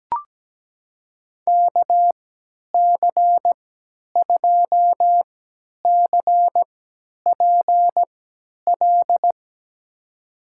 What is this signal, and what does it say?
Morse code, heard as a keyed tone, about 17 wpm, spaced as for 10 wpm, 700 Hz. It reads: KC2CPL
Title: USA Call Signs - 6 Characters - 3x Character Spacing - 17wpm